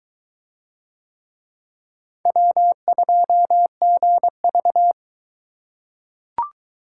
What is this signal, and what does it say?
Morse code, heard as a keyed tone, about 22 wpm, 700 Hz. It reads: W2GV